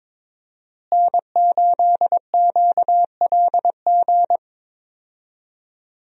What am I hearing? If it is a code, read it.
Morse code, heard as a keyed tone, about 21 wpm, 700 Hz. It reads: N8QLG
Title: USA Call Signs - 5 Characters - Repeated 2x - 22wpm